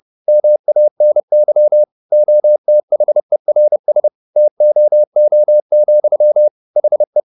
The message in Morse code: MANY OTHERS TOO, HE

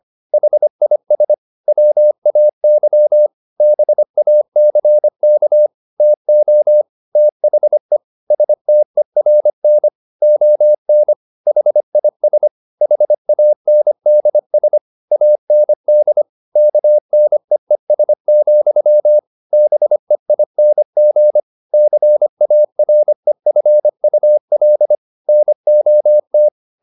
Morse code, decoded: HIS WAY BACK TO THE STERN ON HIS HANDS AND KNEES, BEING CAREFUL NOT

